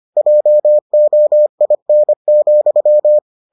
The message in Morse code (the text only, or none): JOIN,